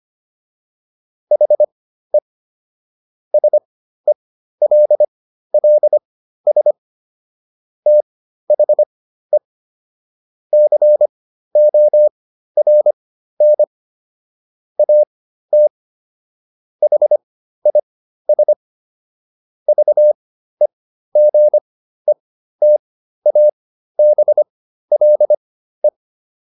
Morse code, heard as a keyed tone, about 25 wpm, spaced as for 13 wpm, 600 Hz. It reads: HE SELLS THE CORN AT HIS VEGETABLE